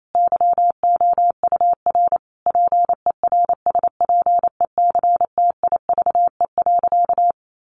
Morse code, heard as a keyed tone, about 28 wpm, 700 Hz. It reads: YOUR PERSPECTIVE.
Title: Sentences from Top 1200 Words 28wpm